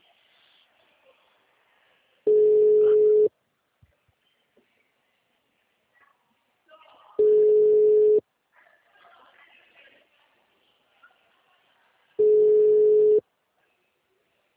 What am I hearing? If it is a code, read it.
Ringback tone (Europe, 425 Hz).
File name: out- -2019